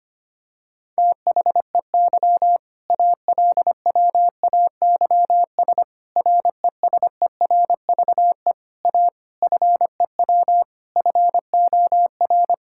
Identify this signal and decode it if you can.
Morse code, heard as a keyed tone, about 25 wpm, 700 Hz. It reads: THEY ALWAYS RESERVE A FEW FOR
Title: Sentences from Top 1000 Words - Review Part 6 - 25wpm